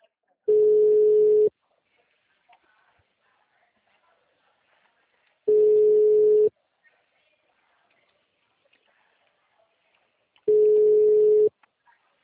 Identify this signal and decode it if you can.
Ringback tone (Europe, 425 Hz).
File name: out- -2007